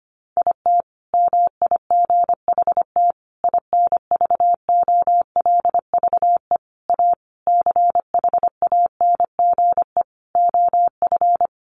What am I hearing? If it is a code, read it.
Morse code, heard as a keyed tone, about 25 wpm, 700 Hz. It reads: IT MIGHT INVOLVE A CHANGE OF
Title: Sentences from Top 1200 Words 25wpm